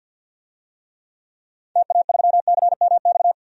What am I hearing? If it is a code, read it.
Morse code, heard as a keyed tone, about 52 wpm, 700 Hz. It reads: TA3CMX
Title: DXCC-100 Call Signs - 6 Characters - 50wpm